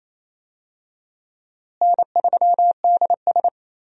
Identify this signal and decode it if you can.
Morse code, heard as a keyed tone, about 27 wpm, 700 Hz. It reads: N3DS
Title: USA Call Signs - 4 Characters - Repeated 2x - 28wpm